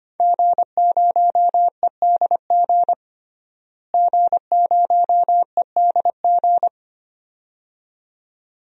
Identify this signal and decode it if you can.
Morse code, heard as a keyed tone, about 25 wpm, 700 Hz. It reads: G0EDG G0EDG
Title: DXCC-100 Call Signs - 5 Characters - Repeated 2x - 25wpm